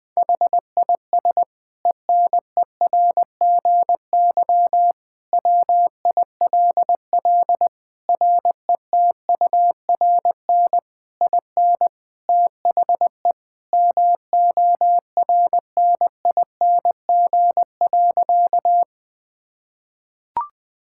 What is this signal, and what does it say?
Morse code, heard as a keyed tone, about 20 wpm, 700 Hz. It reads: HIS ENERGY WILL RETURN IN THE MORNING.
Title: Sentences from Top 1000 Words - Review Part 6 - 20wpm